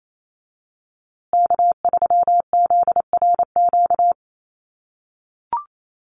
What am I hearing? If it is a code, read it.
Morse code, heard as a keyed tone, about 28 wpm, 700 Hz. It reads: K3ZRQ
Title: USA Call Signs Repeated 3x 28wpm